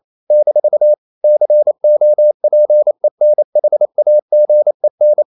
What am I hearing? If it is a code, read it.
Morse code, heard as a keyed tone, about 28 wpm, 600 Hz. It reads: - COPENHAGEN
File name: CW / Practice / NationalCapitals28